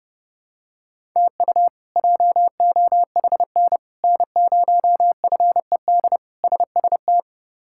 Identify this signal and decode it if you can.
Morse code, heard as a keyed tone, about 30 wpm, 700 Hz. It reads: TU JOHN N0FED SST
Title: K1USN SST QSO Practice - 30wpm